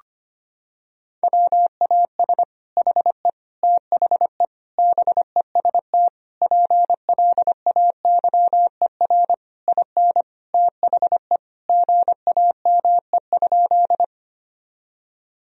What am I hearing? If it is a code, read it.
Morse code, heard as a keyed tone, about 25 wpm, 700 Hz. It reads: WAS HE THE BEST PLAYER IN THE GAME?
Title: Sentences from Top 1000 Words - Review Part 4 - 25wpm